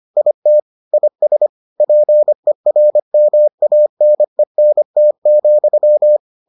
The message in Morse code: IT IS PERMANENT,

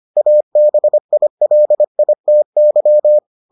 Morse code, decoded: ABILITY

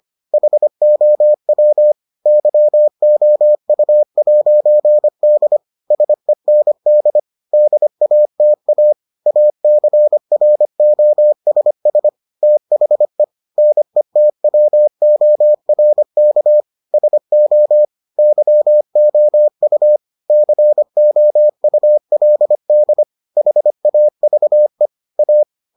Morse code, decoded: HOW YOU'D SEND DATA ACROSS THE NETWORK SO YOU COULD HAVE A

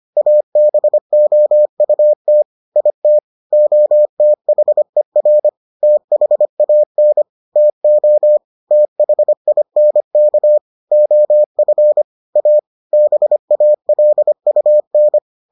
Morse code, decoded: ABOUT IT OTHER THAN TO THINK OF A BALUN

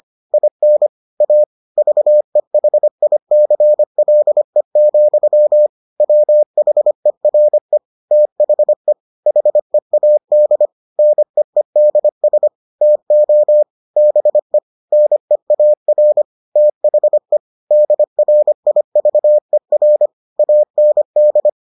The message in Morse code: IN A VEHICLE, WHERE THE HEAD NEEDS TO BE NEAR THE DRIVER AND

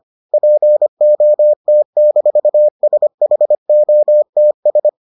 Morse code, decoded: POT-SHOTS

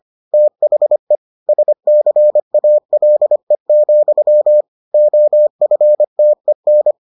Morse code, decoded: THE SCALE, OFTEN

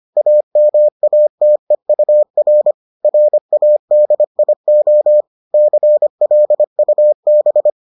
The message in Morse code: AMATEUR RADIO CLUB

